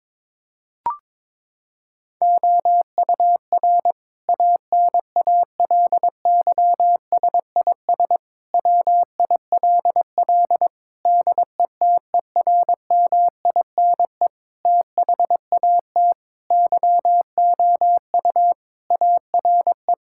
Morse code, heard as a keyed tone, about 22 wpm, 700 Hz. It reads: OUR ANALYSIS WILL DETERMINE THAT YOU ARE
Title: Sentences from Top 1000 Words - Review Part 5 - 22wpm